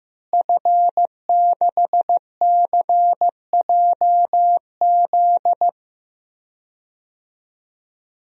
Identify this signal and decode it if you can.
Morse code, heard as a keyed tone, about 15 wpm, 700 Hz. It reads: F6CJZ
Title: DXCC-100 Call Signs - 5 Characters - Repeated 3x - 15wpm